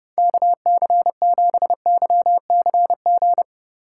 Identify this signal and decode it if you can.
Morse code, heard as a keyed tone, about 29 wpm, 700 Hz. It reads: KC7YCG